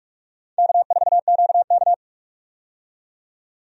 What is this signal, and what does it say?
Morse code, heard as a keyed tone, about 42 wpm, 700 Hz. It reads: K4QK
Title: USA Call Signs - 4 Characters - 45wpm